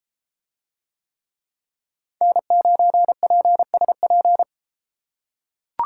Morse code, heard as a keyed tone, about 30 wpm, 700 Hz. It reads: N9PSP